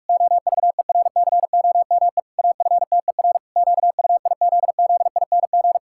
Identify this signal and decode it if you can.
Morse code, heard as a keyed tone, about 41 wpm, 700 Hz. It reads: OVERCOME AFTER QUIZZING
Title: Pangram Challenge v2 - 45wpm